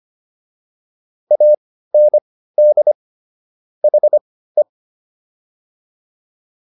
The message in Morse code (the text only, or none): AND HE